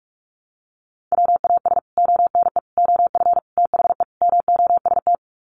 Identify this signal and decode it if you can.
Morse code, heard as a keyed tone, about 44 wpm, 700 Hz. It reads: WAS ONE OF THE MOST
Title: Sets of 5 Words 45wpm